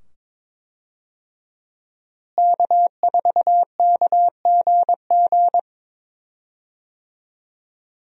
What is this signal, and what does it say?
Morse code, heard as a keyed tone, about 22 wpm, 700 Hz. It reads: K4KGG
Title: USA Call Signs - 5 Characters - Repeated 2x - 22wpm